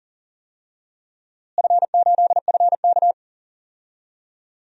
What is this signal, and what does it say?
Morse code, heard as a keyed tone, about 39 wpm, 700 Hz. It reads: F8FK
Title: DXCC-100 Call Signs - 4 Characters - Repeated 2x - 40wpm